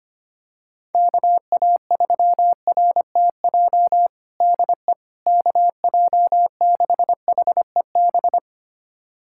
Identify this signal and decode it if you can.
Morse code, heard as a keyed tone, about 25 wpm, 700 Hz. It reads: KA3RTJ DE KJ6HEB
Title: USA Callsign DE Callsign 25wpm